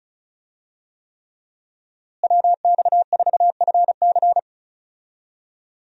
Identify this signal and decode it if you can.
Morse code, heard as a keyed tone, about 36 wpm, 700 Hz. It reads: WX4FC
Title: USA Call Signs - 5 Characters - Repeated 3x - 35wpm